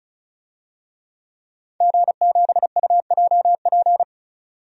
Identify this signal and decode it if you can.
Morse code, heard as a keyed tone, about 36 wpm, 700 Hz. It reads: G7UJP
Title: DXCC-100 Call Signs - 5 Characters - 35wpm